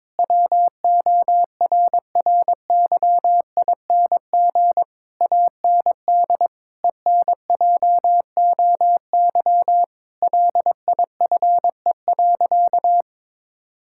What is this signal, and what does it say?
Morse code, heard as a keyed tone, about 22 wpm, 700 Hz. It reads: WORRYING AND ENJOY LIFE.
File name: practice / 22wpm / Sentences with 8-Word Phrases - 22wpm